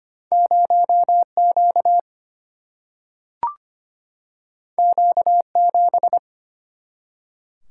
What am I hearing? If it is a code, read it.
Morse code, heard as a keyed tone, about 25 wpm, 700 Hz. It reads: 0 Q Q 7
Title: Single-Letter Single-Number 25wpm